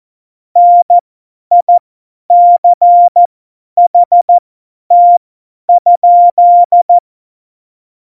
Morse code, heard as a keyed tone, about 14 wpm, spaced as for 10 wpm, 700 Hz. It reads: NICHT?
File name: Morsekurs / Witze / Witz 21. November 2017, 50 F70